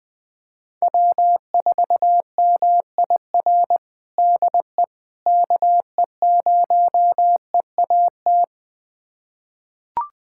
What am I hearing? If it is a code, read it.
Morse code, heard as a keyed tone, about 20 wpm, 700 Hz. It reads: W4MIR DE KE0EAT
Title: USA Callsign DE Callsign 20wpm